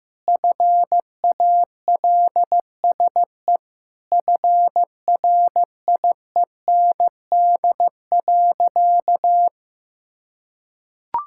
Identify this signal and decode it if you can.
Morse code, heard as a keyed tone, about 15 wpm, 700 Hz. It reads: FALSE FRIEND.
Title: Sentences from Top 1600 Words 15wpm